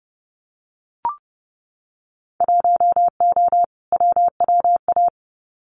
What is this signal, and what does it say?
Morse code, heard as a keyed tone, about 30 wpm, 700 Hz. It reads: 1O WWA
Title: Winter Field Day Exchanges 30wpm